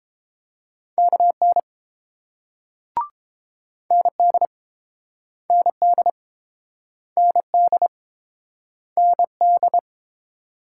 Morse code, heard as a keyed tone, about 31 wpm, 700 Hz. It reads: KN ND ND ND ND